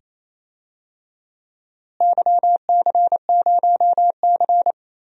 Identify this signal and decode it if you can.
Morse code, heard as a keyed tone, about 28 wpm, 700 Hz. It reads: YC0C
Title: DXCC-100 Call Signs - 4 Characters - Repeated 2x - 28wpm